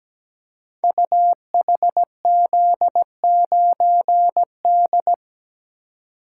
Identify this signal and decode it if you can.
Morse code, heard as a keyed tone, about 17 wpm, 700 Hz. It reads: UHZ9D
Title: Code Groups - 5 Letter:Numbers - 17wpm